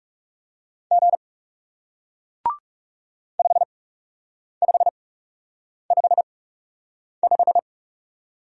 Morse code, heard as a keyed tone, about 40 wpm, spaced as for 8 wpm, 700 Hz. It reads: G 5555